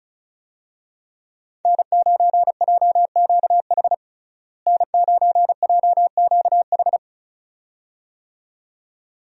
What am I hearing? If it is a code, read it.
Morse code, heard as a keyed tone, about 36 wpm, 700 Hz. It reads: N9JQH N9JQH
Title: USA Call Signs - 5 Characters - Repeated 2x - 35wpm